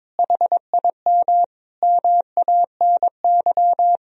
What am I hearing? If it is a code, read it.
Morse code, heard as a keyed tone, about 22 wpm, 700 Hz. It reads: HIM MANY
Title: Sentences from Top 200 Words and Abbreviations 22wpm